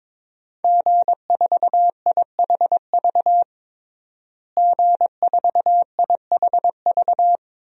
Morse code, heard as a keyed tone, about 22 wpm, 700 Hz. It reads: G4IHV G4IHV
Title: DXCC-100 Call Signs - 5 Characters - Repeated 2x - 22wpm